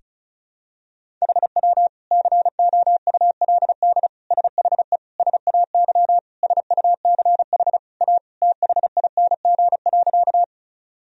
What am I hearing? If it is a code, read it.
Morse code, heard as a keyed tone, about 36 wpm, 700 Hz. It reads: HW COULD SHE SAY SUCH A THING.
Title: Sentences from Top 200 Words and Abbreviations 35wpm